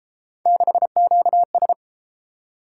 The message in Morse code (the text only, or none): 6QS